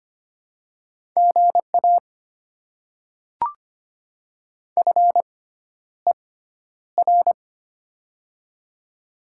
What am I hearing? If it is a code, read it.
Morse code, heard as a keyed tone, about 25 wpm, spaced as for 11 wpm, 700 Hz. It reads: GA FER
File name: Top 20 QSO Elements - 6x Character Spacing - 25wpm